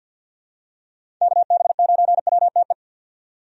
Morse code, heard as a keyed tone, about 53 wpm, 700 Hz. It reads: KB9WTE
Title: USA Call Signs - 6 Characters - 50wpm